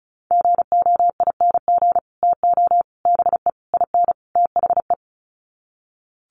Morse code, heard as a keyed tone, about 34 wpm, 700 Hz. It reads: GOING TO BE IN THE
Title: Sets of 5 Words Encore 35wpm